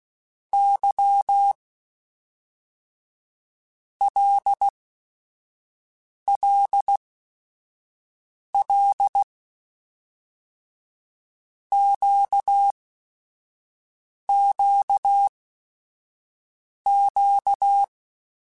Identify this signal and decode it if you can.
Morse code, heard as a keyed tone, about 16 wpm, 800 Hz. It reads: Y L L L Q Q Q
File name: Basic Sets / Basic Letter Groups Collective7 / LFCYQ@5